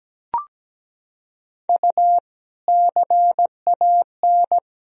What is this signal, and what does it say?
Morse code, heard as a keyed tone, about 17 wpm, 700 Hz. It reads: U CAN
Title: Sentences from Top 500 Words and Abbreviations 17wpm